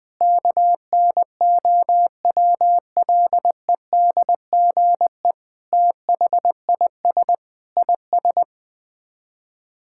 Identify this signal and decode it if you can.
Morse code, heard as a keyed tone, about 21 wpm, 700 Hz. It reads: KNOWLEDGE THIS IS
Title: Sets of 8 Words - 20wpm